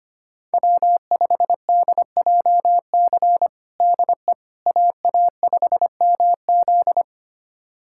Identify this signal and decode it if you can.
Morse code, heard as a keyed tone, about 25 wpm, 700 Hz. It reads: W5DJC DE AA5MZ
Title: USA Callsign DE Callsign - 5 Characters - 25wpm